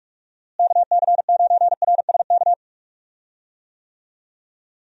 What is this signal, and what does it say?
Morse code, heard as a keyed tone, about 46 wpm, 700 Hz. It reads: KC9RSK
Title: USA Call Signs - 6 Characters - Repeated 2x - 45wpm